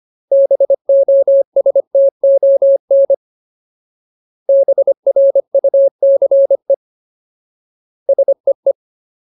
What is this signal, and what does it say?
Morse code, heard as a keyed tone, about 25 wpm, 550 Hz. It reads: BOSTON BRUCE SEE
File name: Ultimate Word List / 25WPM Ultimate